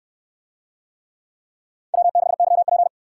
Morse code, heard as a keyed tone, about 59 wpm, 700 Hz. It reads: W6YP